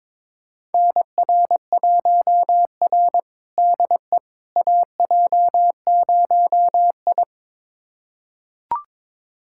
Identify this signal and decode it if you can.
Morse code, heard as a keyed tone, about 22 wpm, 700 Hz. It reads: NR1R DE AJ0I
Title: USA Callsign DE Callsign - 4 Characters - 22wpm